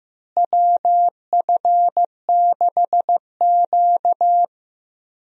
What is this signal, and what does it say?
Morse code, heard as a keyed tone, about 15 wpm, 700 Hz. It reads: WF6Q
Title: USA Call Signs - 4 Characters - Repeated 3x - 15wpm